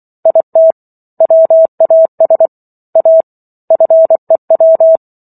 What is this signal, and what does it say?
Morse code, heard as a keyed tone, about 24 wpm, 650 Hz. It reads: IT WAS A FEW